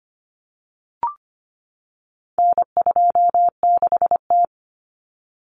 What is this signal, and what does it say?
Morse code, heard as a keyed tone, about 25 wpm, 700 Hz. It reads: N26T
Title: Common Cut Numbers - 4-Digit 25wpm